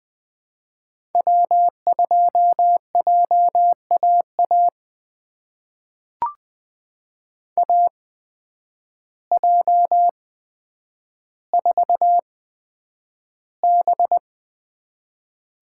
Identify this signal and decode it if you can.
Morse code, heard as a keyed tone, about 20 wpm, spaced as for 9 wpm, 700 Hz. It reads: W2JAA AJ4B
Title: USA Call Signs - 5 Characters - 8x Character Spacing - 20wpm